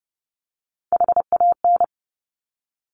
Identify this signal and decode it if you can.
Morse code, heard as a keyed tone, about 31 wpm, 700 Hz. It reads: HAN
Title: Top Three Letter Combinations in Words 30wpm